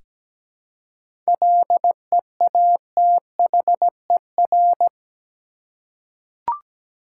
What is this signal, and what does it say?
Morse code, heard as a keyed tone, about 17 wpm, 700 Hz. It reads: LEATHER
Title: Seven Letter Words - 17wpm